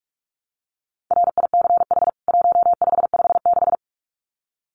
Morse code, heard as a keyed tone, about 44 wpm, 700 Hz. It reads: RICH 1556
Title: CWT Exchange 45wpm